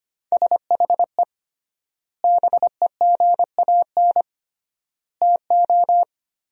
Morse code, heard as a keyed tone, about 25 wpm, 700 Hz. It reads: SHE BEGAN TO
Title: 3x Word-Spacing - Sentences from Top 500 Words - 25wpm